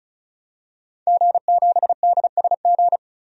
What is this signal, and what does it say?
Morse code, heard as a keyed tone, about 34 wpm, 700 Hz. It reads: G7DSG
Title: DXCC-100 Call Signs - 5 Characters - 35wpm